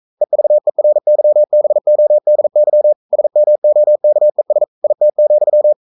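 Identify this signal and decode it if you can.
Morse code, heard as a keyed tone, about 42 wpm, 600 Hz. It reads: EVERYBODY SMOKES IT,
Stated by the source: Morse code sending of an English novel at 42 wpm.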